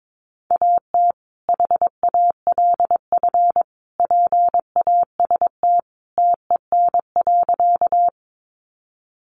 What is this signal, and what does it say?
Morse code, heard as a keyed tone, about 22 wpm, 700 Hz. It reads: AT HALF PAST TEN.
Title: Sentences from Top 600 Words 22wpm